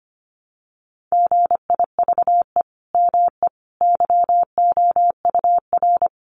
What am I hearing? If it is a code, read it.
Morse code, heard as a keyed tone, about 25 wpm, 700 Hz. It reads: GIVE ME YOUR
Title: Sentences from Top 200 Words 25wpm